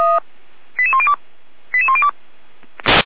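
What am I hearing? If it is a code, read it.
Five-tone selective call (ZVEI), heard as quick runs of five tones, tones of about 70 ms. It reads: 88102 88102
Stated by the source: ZVEI five-tone selective call 88103 received by SDR at feuerwehr-orlamuende.org, fire brigade.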